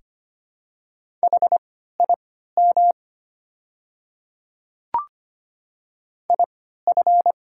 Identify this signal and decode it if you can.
Morse code, heard as a keyed tone, about 25 wpm, spaced as for 14 wpm, 700 Hz. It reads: HIM IF
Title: 3x Character Spacing - Top 100 Words - 25wpm